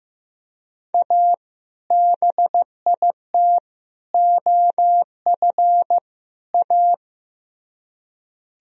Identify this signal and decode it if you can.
Morse code, heard as a keyed tone, about 18 wpm, 700 Hz. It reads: A BIT OF A
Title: Speed-Racing - Sets of 4 Words 15wpm